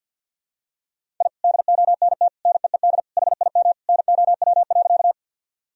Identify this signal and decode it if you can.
Morse code, heard as a keyed tone, about 52 wpm, 700 Hz. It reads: I DONT NEED HIM NOW.
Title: Sentences from Top 1000 Words - Review Part 1 - 50wpm